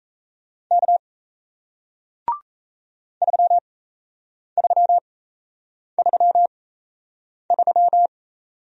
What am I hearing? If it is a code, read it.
Morse code, heard as a keyed tone, about 38 wpm, spaced as for 8 wpm, 700 Hz. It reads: K 3333